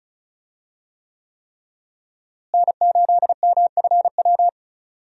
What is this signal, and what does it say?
Morse code, heard as a keyed tone, about 34 wpm, 700 Hz. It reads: N8MFW
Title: USA Call Signs - 5 Characters - Repeated 3x - 35wpm